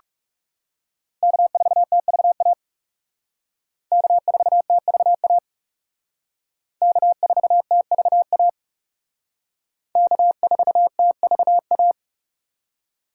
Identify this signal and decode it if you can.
Morse code, heard as a keyed tone, about 36 wpm, 700 Hz. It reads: K4TVA K4TVA K4TVA K4TVA
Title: Speed Racing - USA 1x3 Call Signs 30wpm